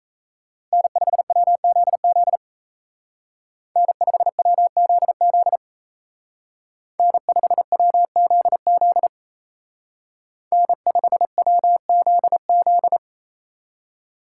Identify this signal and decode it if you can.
Morse code, heard as a keyed tone, about 36 wpm, 700 Hz. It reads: N5WZZ N5WZZ N5WZZ N5WZZ